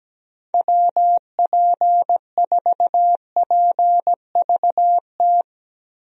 Morse code, heard as a keyed tone, about 17 wpm, 700 Hz. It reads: WP4PVT
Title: USA Call Signs - 6 Characters - Repeated 2x - 17wpm